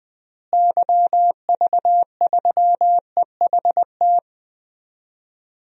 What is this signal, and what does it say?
Morse code, heard as a keyed tone, about 21 wpm, 700 Hz. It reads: YV3EHT
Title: DXCC-100 Call Signs - 6 Characters - Repeated 3x - 20wpm